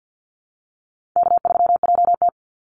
Morse code, heard as a keyed tone, about 48 wpm, 700 Hz. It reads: K3JT
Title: Easy Call Signs Repeated 2x - 50wpm